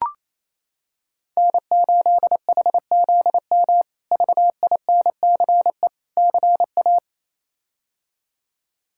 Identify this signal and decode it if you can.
Morse code, heard as a keyed tone, about 28 wpm, 700 Hz. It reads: N8HZM VINCE CA
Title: K1USN SST QSO Practice - 28wpm